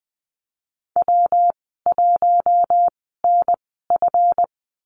Morse code, heard as a keyed tone, about 20 wpm, spaced as for 15 wpm, 700 Hz. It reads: W1NF
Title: 245 Lesson 39g - Letter z - Call Signs 20wpm